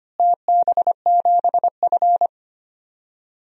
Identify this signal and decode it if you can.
Morse code, heard as a keyed tone, about 25 wpm, 700 Hz. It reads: TB7F
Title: DXCC-100 Call Signs - 4 Characters - Repeated 2x - 25wpm